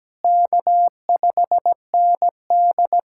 Morse code, heard as a keyed tone, about 17 wpm, 700 Hz. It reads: K5ND
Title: USA Callsign DE Callsign - 4 Characters - 17wpm